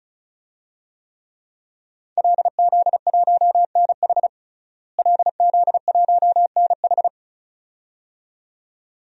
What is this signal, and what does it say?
Morse code, heard as a keyed tone, about 34 wpm, 700 Hz. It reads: LZ1NH LZ1NH
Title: DXCC-100 Call Signs - 5 Characters - Repeated 2x - 35wpm